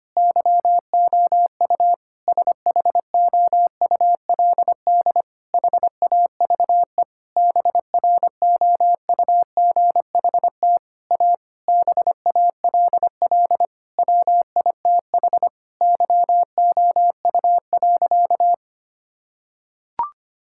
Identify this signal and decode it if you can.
Morse code, heard as a keyed tone, about 25 wpm, 700 Hz. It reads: YOU SHOULD HAVE BROUGHT A BALL WITH YOU.
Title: Sentences from Top 1000 Words - Review Part 6 - 25wpm